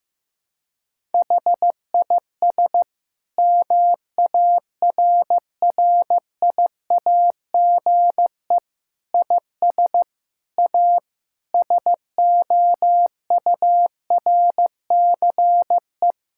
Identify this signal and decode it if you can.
Morse code, heard as a keyed tone, about 15 wpm, 700 Hz. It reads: HIS MARRIAGE IS A SOURCE
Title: Sentences from Top 1000 Words - Review Part 6 - 15wpm